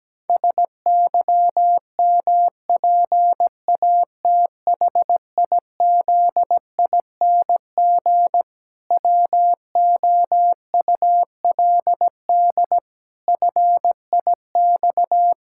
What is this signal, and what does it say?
Morse code, heard as a keyed tone, about 17 wpm, 700 Hz. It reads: SYMPATHIZING WOULD FIX
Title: Pangram Challenge v2 - 17wpm